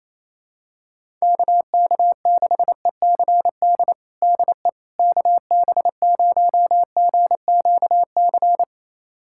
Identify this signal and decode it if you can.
Morse code, heard as a keyed tone, about 28 wpm, 700 Hz. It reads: KK6ECD DE KB0GQC
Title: USA Callsign DE Callsign - 6 Characters - 28wpm